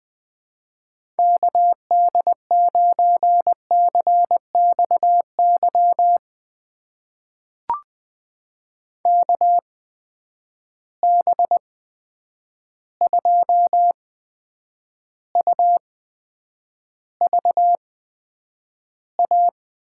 Morse code, heard as a keyed tone, about 20 wpm, spaced as for 9 wpm, 700 Hz. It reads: KD9CXY KB2UVA
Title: USA Call Signs - 6 Characters - 8x Character Spacing - 20wpm